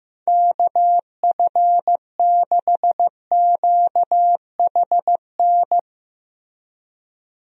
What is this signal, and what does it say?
Morse code, heard as a keyed tone, about 15 wpm, 700 Hz. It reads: KF6QHN
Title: USA Call Signs - 6 Characters - 15wpm